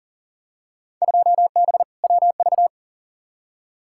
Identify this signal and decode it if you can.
Morse code, heard as a keyed tone, about 40 wpm, 700 Hz. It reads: 2B WV
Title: Field Day Exchanges Encore - 40wpm